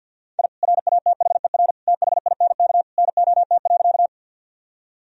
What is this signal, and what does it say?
Morse code, heard as a keyed tone, about 50 wpm, 700 Hz. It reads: I RATHER THINK NOT.
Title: Sentences from Top 1000 Words - Review Part 4 - 50wpm